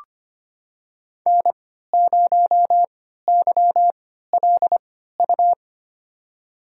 Morse code, heard as a keyed tone, about 25 wpm, spaced as for 14 wpm, 700 Hz. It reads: N0YLU
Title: USA Call Signs - 5 Characters - 3x Character Spacing - 25wpm